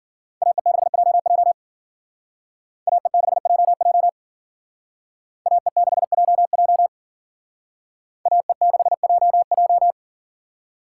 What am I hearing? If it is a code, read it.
Morse code, heard as a keyed tone, about 51 wpm, 700 Hz. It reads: AE6JJ AE6JJ AE6JJ AE6JJ